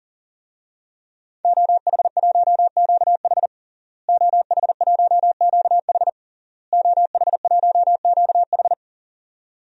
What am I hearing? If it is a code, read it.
Morse code, heard as a keyed tone, about 40 wpm, 700 Hz. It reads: OH1QH OH1QH OH1QH
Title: DXCC-100 Call Signs - 5 Characters - Repeated 3x - 40wpm